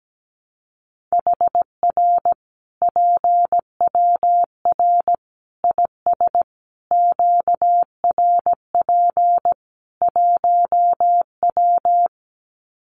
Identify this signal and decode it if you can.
Morse code, heard as a keyed tone, about 17 wpm, 700 Hz. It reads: HR PWR IS QRP 1W
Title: Common QSO Phrases 17wpm